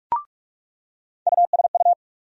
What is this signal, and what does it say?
Morse code, heard as a keyed tone, about 46 wpm, 700 Hz. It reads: USU
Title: USA Call Sign Postfixes - 3 Letters 45wpm